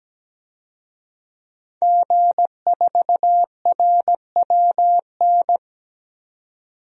Morse code, heard as a keyed tone, about 17 wpm, 700 Hz. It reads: G4RWN
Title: DXCC-100 Call Signs - 5 Characters - 17wpm